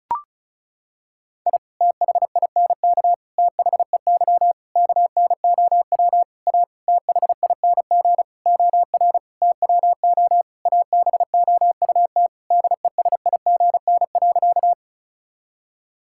Morse code, heard as a keyed tone, about 34 wpm, 700 Hz. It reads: I THINK THEY KNOW A THING OR TWO ABOUT DESIGN.
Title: Sentences from Top 1000 Words - Review Part 5 - 35wpm